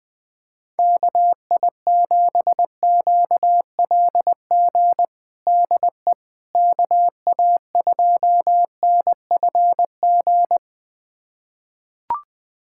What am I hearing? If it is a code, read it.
Morse code, heard as a keyed tone, about 21 wpm, 700 Hz. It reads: KI7QLG DE KA2NFG
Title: USA Callsign DE Callsign - 6 Characters - 20wpm